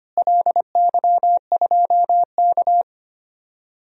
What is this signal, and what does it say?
Morse code, heard as a keyed tone, about 25 wpm, 700 Hz. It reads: LY2K
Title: DXCC-100 Call Signs - 4 Characters - Repeated 3x - 25wpm